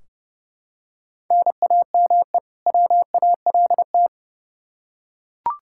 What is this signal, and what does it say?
Morse code, heard as a keyed tone, about 30 wpm, 700 Hz. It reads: NAME WALT
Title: Practice QSO - Exchange 1 - Beginning - 30wpm